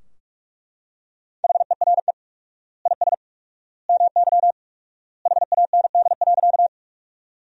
Morse code, heard as a keyed tone, about 41 wpm, 700 Hz. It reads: HERE IS MY HAND.
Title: 4x Word-Spacing - Sentences from Top 200 Words - 45wpm